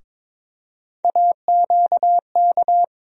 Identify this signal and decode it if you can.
Morse code, heard as a keyed tone, about 22 wpm, 700 Hz. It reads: AQK
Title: USA Call Sign Postfixes - 3 Letters 22wpm